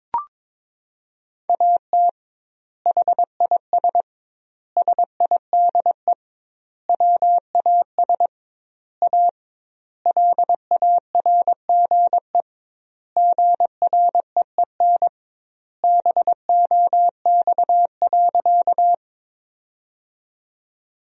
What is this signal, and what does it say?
Morse code, heard as a keyed tone, about 22 wpm, 700 Hz. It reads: AT HIS SIDE WAS A LARGE GREEN BOX.
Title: 2x Word-Spacing - Sentences from Top 500 Words - 22wpm